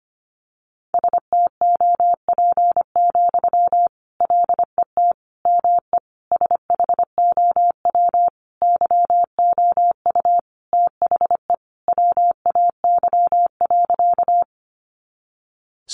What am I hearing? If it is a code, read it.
Morse code, heard as a keyed tone, about 25 wpm, 700 Hz. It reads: STOP, LET ME SHOW YOU THE WAY.
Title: Sentences from Top 300 Words 25wpm